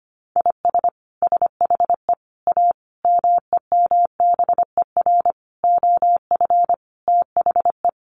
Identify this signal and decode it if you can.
Morse code, heard as a keyed tone, about 25 wpm, 700 Hz. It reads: IS SHE A MEMBER OF THE
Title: Sentences from Top 1100 Words 25wpm